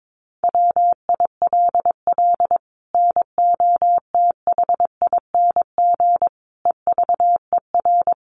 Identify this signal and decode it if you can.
Morse code, heard as a keyed tone, about 22 wpm, 700 Hz. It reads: WILL NOTHING EVER
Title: Sentences from Top 1200 Words 22wpm